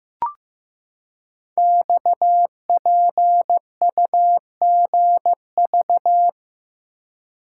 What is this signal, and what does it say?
Morse code, heard as a keyed tone, about 15 wpm, 700 Hz. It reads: XPUGV
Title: Code Groups - 5 Letters - 15wpm